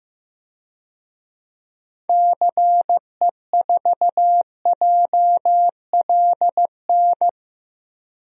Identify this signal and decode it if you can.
Morse code, heard as a keyed tone, about 15 wpm, 700 Hz. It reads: CE4JLN
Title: DXCC-100 Call Signs - 6 Characters - Repeated 3x - 15wpm